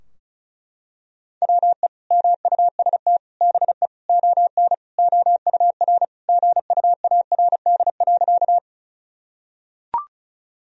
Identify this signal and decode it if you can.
Morse code, heard as a keyed tone, about 36 wpm, 700 Hz. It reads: WE MUST BE ON OUR GUARD.
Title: Sentences from Top 1000 Words - Review Part 4 - 35wpm